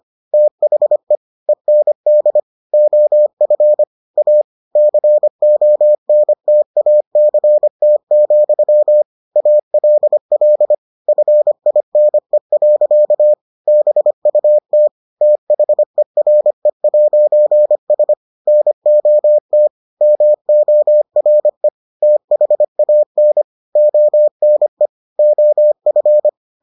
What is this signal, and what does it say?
Morse code, heard as a keyed tone, about 25 wpm, 600 Hz. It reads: THE END OF A CONTACT, ALL FINE. BUT THERE'S NOT MORE THAN ONE OF